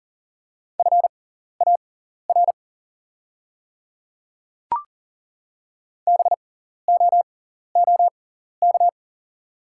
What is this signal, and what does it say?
Morse code, heard as a keyed tone, about 38 wpm, spaced as for 14 wpm, 700 Hz. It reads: FAR BOOK
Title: 6x Character Spacing - Top 300 Words - 40wpm